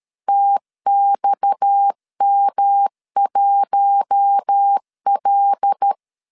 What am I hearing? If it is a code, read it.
Morse code, heard as a keyed tone, about 13 wpm, 800 Hz. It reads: TXM1L
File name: LEZIONI / 7 GF9Z / 7 GF9Z 12wpm 2147